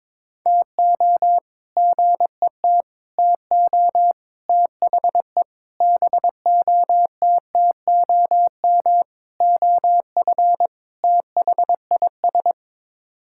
Morse code, heard as a keyed tone, about 21 wpm, 700 Hz. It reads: TO GET TO THE BOTTOM OF THIS
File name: practice / 22wpm / Sets of 7 Words - 22wpm